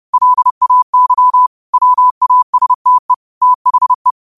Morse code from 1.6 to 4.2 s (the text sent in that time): WASTE THE